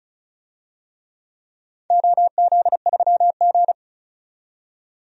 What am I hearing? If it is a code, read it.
Morse code, heard as a keyed tone, about 36 wpm, 700 Hz. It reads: OZ3G
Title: DXCC-100 Call Signs - 4 Characters - 35wpm